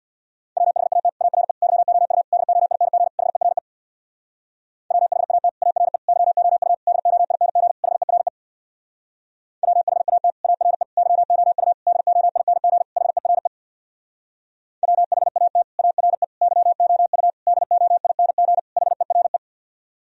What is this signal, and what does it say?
Morse code, heard as a keyed tone, about 64 wpm, 700 Hz. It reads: WHAT ARE YOU DOING HERE WHAT ARE YOU DOING HERE WHAT ARE YOU DOING HERE WHAT ARE YOU DOING HERE